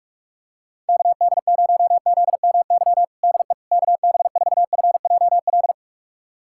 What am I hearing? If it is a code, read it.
Morse code, heard as a keyed tone, about 41 wpm, 700 Hz. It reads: KD0ZMY DE KB4FJL
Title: USA Callsign DE Callsign - 6 Characters - 45wpm